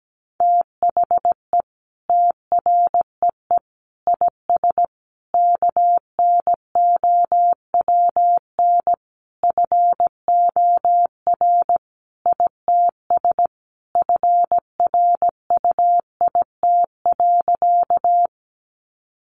Morse code, heard as a keyed tone, about 17 wpm, 700 Hz. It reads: THE TREE IS KNOWN FOR ITS FRUIT.
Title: Sentences from Top 1300 Words 17wpm